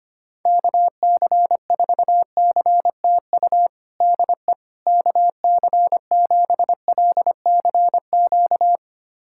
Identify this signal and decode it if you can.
Morse code, heard as a keyed tone, about 25 wpm, 700 Hz. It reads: KC4CTU DE KC7LCQ
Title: USA Callsign DE Callsign - 6 Characters - 25wpm